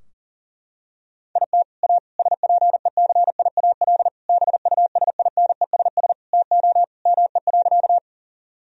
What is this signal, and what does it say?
Morse code, heard as a keyed tone, about 40 wpm, 700 Hz. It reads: IT A SPECIAL BUSINESS TO ME.
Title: Sentences from Top 1000 Words - Review Part 3 - 40wpm